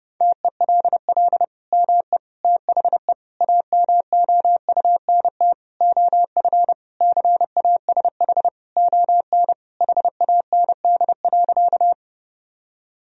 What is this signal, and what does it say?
Morse code, heard as a keyed tone, about 28 wpm, 700 Hz. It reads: TELL ME THE AMOUNT OF CASH ON HAND.
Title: Sentences from Top 1000 Words - Review Part 4 - 30wpm